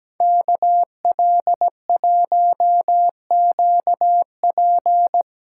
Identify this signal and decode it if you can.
Morse code, heard as a keyed tone, about 19 wpm, 700 Hz. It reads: KL1QP
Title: Speed Racing - USA 2x2 Call Signs 17wpm